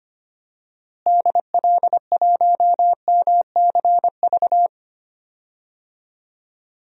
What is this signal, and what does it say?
Morse code, heard as a keyed tone, about 25 wpm, 700 Hz. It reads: DL1MCV
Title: DXCC-100 Call Signs - 6 Characters - Repeated 3x - 25wpm